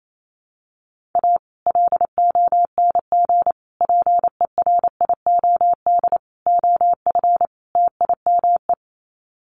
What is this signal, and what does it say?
Morse code, heard as a keyed tone, about 28 wpm, 700 Hz. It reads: A LONG PERIOD OF TIME
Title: Sets of 5 Words 28wpm